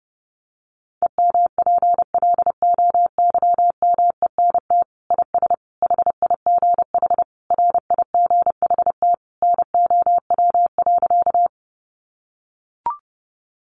Morse code, heard as a keyed tone, about 30 wpm, 700 Hz. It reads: EMPLOYMENT IS HIGH RIGHT NOW.
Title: Sentences from Top 1500 Words 30wpm